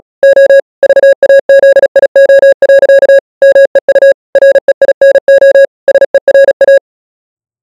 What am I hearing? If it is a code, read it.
Morse code, heard as a keyed tone, about 37 wpm, 550 Hz. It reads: O VAZIO. MEU REINO SERA